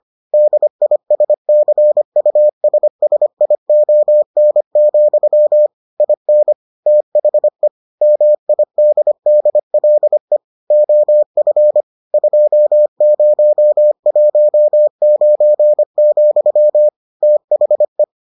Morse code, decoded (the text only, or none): DISCUSSION, IN THE MIDDLE OF 2019, THE